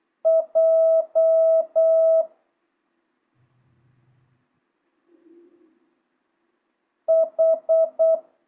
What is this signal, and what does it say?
Morse code, heard as a keyed tone, about 8 wpm, 650 Hz. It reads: J H
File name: jh8cbh / 2025cw